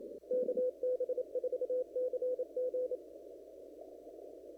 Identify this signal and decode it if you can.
Morse code, heard as a keyed tone, about 27 wpm, 500 Hz. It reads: KB4CG
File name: CW / kt5j wpx cw-op-k5pi